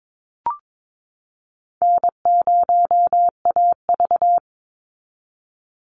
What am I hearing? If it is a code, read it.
Morse code, heard as a keyed tone, about 22 wpm, 700 Hz. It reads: N0AV
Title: Easy Call Signs Repeated 3x - 22wpm